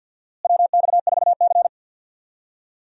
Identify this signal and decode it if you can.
Morse code, heard as a keyed tone, about 46 wpm, 700 Hz. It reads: WX4C
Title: USA Call Signs - 4 Characters - Repeated 2x - 50wpm